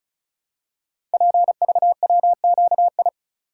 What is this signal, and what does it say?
Morse code, heard as a keyed tone, about 36 wpm, 700 Hz. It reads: PVWQI
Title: Code Groups - 5 Letters - 35wpm